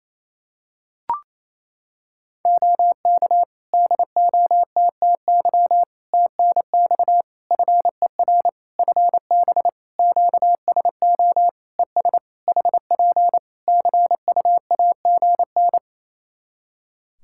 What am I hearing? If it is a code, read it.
Morse code, heard as a keyed tone, about 27 wpm, 700 Hz. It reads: OK DOTTY TNX FER FB QSO ES HP CUAGN
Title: K1USN SST QSO Practice 28wpm